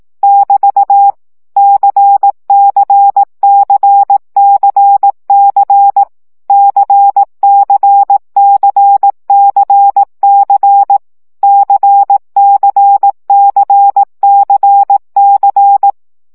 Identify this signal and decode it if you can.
Morse code, heard as a keyed tone, about 18 wpm, 800 Hz. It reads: <BT> CCCCC CCCCC CCCCC
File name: CW gyakorlas c 15 baud